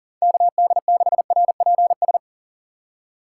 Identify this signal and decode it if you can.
Morse code, heard as a keyed tone, about 40 wpm, 700 Hz. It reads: KD6RPS
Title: USA Call Signs - 6 Characters - 6x Character Spacing - 40wpm